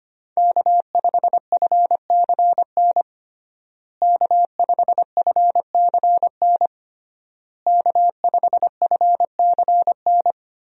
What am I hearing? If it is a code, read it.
Morse code, heard as a keyed tone, about 25 wpm, 700 Hz. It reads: K5FCN K5FCN K5FCN
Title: USA Call Signs - 5 Characters - Repeated 3x - 25wpm